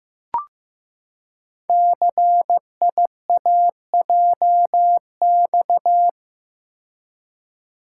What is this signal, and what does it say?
Morse code, heard as a keyed tone, about 15 wpm, 700 Hz. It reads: CIAJX
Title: Code Groups - 5 Letters - 15wpm